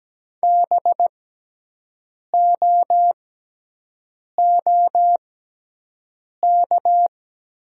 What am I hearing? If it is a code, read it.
Morse code, heard as a keyed tone, about 17 wpm, spaced as for 6 wpm, 700 Hz. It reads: BOOK
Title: 6x Character Spacing - Top 500 Words - 17wpm